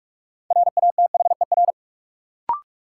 Morse code, heard as a keyed tone, about 41 wpm, 700 Hz. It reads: RATHER
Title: Sentences from Top 700 Words - New Word Drill - 45wpm